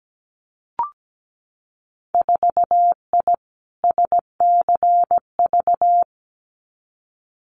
Morse code, heard as a keyed tone, about 17 wpm, 700 Hz. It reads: 4I SCV
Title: Winter Field Day Exchanges Encore 17wpm